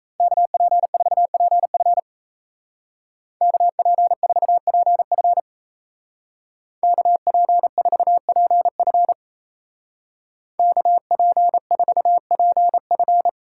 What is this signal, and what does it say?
Morse code, heard as a keyed tone, about 35 wpm, 700 Hz. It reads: KP4PF KP4PF KP4PF KP4PF
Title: Speed Racing - USA 2x2 Call Signs 28wpm